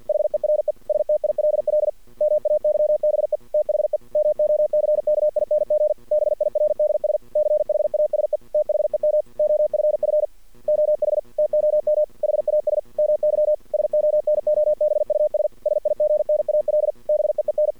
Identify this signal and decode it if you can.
Morse code, heard as a keyed tone, about 50 wpm, 600 Hz. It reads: HERE STILL MAYBE THE MOCKING BIRDS CHASE THEM OFF OH TOM HAS MY SONOBUS SIGNAL BEEN
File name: archived / CRSnet